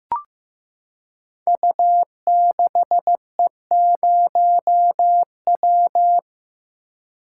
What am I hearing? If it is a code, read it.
Morse code, heard as a keyed tone, about 15 wpm, 700 Hz. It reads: U6E0W
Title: Code Groups - 5 Letter:Numbers - 15wpm